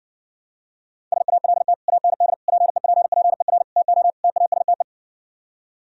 Morse code, heard as a keyed tone, about 60 wpm, 700 Hz. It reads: SALT AND PEPPER TO TASTE